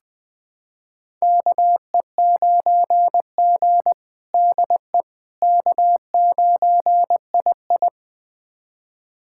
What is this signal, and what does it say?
Morse code, heard as a keyed tone, about 20 wpm, 700 Hz. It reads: KE9G DE K9II
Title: USA Callsign DE Callsign - 4 Characters - 20wpm